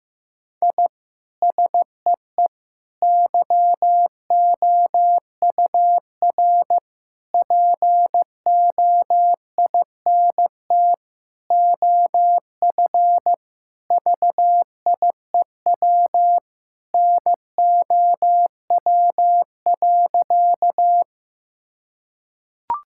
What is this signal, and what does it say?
Morse code, heard as a keyed tone, about 15 wpm, 700 Hz. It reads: I SEE YOUR POINT OF VIEW NOW.
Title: Sentences from Top 1000 Words - Review Part 6 - 15wpm